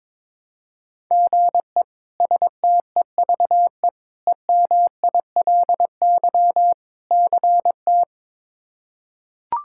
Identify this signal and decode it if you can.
Morse code, heard as a keyed tone, about 21 wpm, 700 Hz. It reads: GE STEVE EMILY CT
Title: K1USN SST QSO Practice - 22wpm